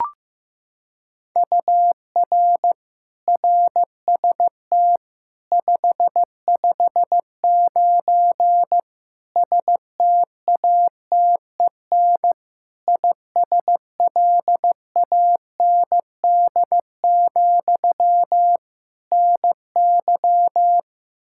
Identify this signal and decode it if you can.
Morse code, heard as a keyed tone, about 15 wpm, 700 Hz. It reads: UR RST 559 STATEN ISLAND, NY
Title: Practice QSO - Exchange 1 - Beginning - 15wpm